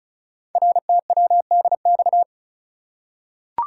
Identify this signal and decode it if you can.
Morse code, heard as a keyed tone, about 36 wpm, 700 Hz. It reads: RTWDX
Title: Code Groups - 5 Letters - 35wpm